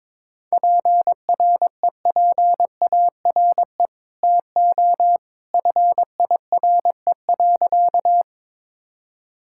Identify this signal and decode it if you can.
Morse code, heard as a keyed tone, about 22 wpm, 700 Hz. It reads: PREPARE TO FIRE.
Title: Sentences from Top 1000 Words - Review Part 6 - 22wpm